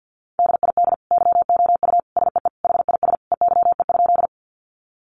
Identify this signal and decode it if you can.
Morse code, heard as a keyed tone, about 48 wpm, 700 Hz. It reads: DID YOU SEE HIS EYE?
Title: Sentences from Top 300 Words 50wpm